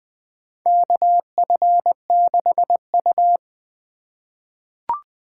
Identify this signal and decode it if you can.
Morse code, heard as a keyed tone, about 20 wpm, 700 Hz. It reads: KF6U
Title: USA Call Signs - 4 Characters - 3x Character Spacing - 20wpm